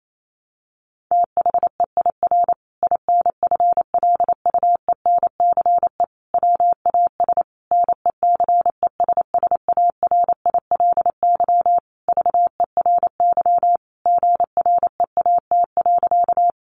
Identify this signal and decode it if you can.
Morse code, heard as a keyed tone, about 28 wpm, 700 Hz. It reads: THEIR INFLUENCE WAS NECESSARILY VERY GREAT.
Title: Sentences from Top 1400 Words 28wpm